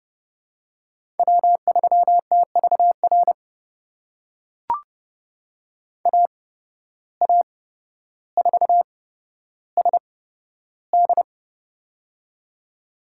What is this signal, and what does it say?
Morse code, heard as a keyed tone, about 31 wpm, spaced as for 13 wpm, 700 Hz. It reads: W3TVR AA4SD
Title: USA Call Signs - 5 Characters - 8x Character Spacing - 30wpm